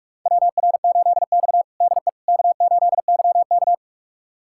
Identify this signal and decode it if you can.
Morse code, heard as a keyed tone, about 46 wpm, 700 Hz. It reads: WR8X DE K8YK
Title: USA Callsign DE Callsign - 4 Characters - 45wpm